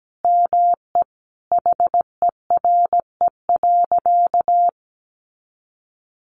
Morse code, heard as a keyed tone, about 17 wpm, 700 Hz. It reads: ME HERE.
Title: Sentences from Top 200 Words 17wpm